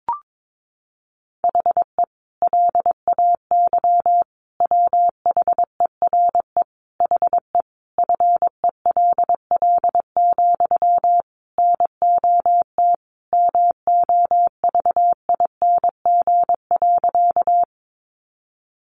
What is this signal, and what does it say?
Morse code, heard as a keyed tone, about 22 wpm, 700 Hz. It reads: HE LAY WHERE HE FELL, NOT MOVING.
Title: A Sound Of Thunder 22wpm